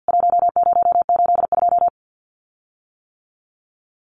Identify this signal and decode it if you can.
Morse code, heard as a keyed tone, about 48 wpm, 700 Hz. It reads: 1082
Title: CWT Exchange 50wpm